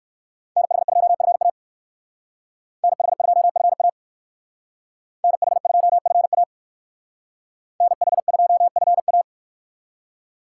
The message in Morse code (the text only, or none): NH2FA NH2FA NH2FA NH2FA